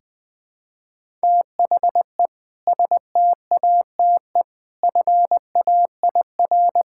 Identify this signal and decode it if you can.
Morse code, heard as a keyed tone, about 19 wpm, 700 Hz. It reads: THE STATE FAIR
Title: Sentences with 2-Word Phrases - 20wpm